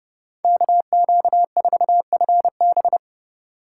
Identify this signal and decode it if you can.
Morse code, heard as a keyed tone, about 30 wpm, 700 Hz. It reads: KQ4FB
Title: USA Call Signs - 5 Characters - 2x Character Spacing - 30wpm